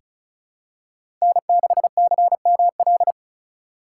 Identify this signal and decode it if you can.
Morse code, heard as a keyed tone, about 36 wpm, 690 Hz. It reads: N6CML
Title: USA Call Signs - 5 Characters - 3x Character Spacing - 35wpm